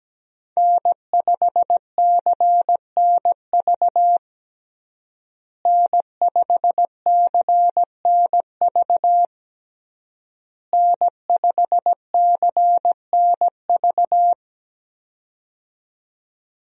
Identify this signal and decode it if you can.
Morse code, heard as a keyed tone, about 17 wpm, 700 Hz. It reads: N5CNV N5CNV N5CNV
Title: USA Call Signs - 5 Characters - Repeated 3x - 17wpm